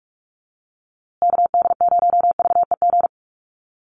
Morse code, heard as a keyed tone, about 44 wpm, 700 Hz. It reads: KD0VEG
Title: USA Call Signs Repeated 2x 45wpm